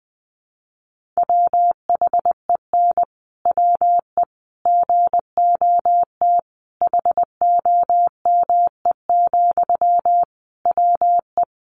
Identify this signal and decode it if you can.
Morse code, heard as a keyed tone, about 20 wpm, 700 Hz. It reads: WHEN WE GOT HOME, WE
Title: Sentences from Top 1400 Words 20wpm